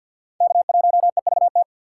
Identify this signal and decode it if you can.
Morse code, heard as a keyed tone, about 64 wpm, 700 Hz. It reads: K1EVT